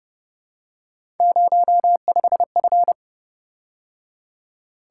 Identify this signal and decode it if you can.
Morse code, heard as a keyed tone, about 28 wpm, 700 Hz. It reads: 05F